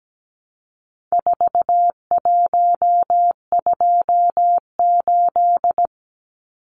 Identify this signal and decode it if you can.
Morse code, heard as a keyed tone, about 17 wpm, 700 Hz. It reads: 4128
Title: Numbers - 4-Digit 17wpm